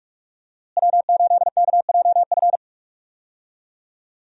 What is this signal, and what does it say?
Morse code, heard as a keyed tone, about 42 wpm, 700 Hz. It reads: W8KJF
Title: USA Call Signs - 5 Characters - Repeated 3x - 45wpm